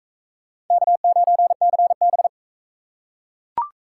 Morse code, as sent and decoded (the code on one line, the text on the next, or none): -.- ----. -.-. -...
K9CB